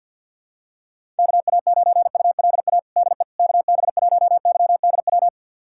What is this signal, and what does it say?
Morse code, heard as a keyed tone, about 46 wpm, 700 Hz. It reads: KA9ULA DE KB1YDW
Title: USA Callsign DE Callsign - 6 Characters - 50wpm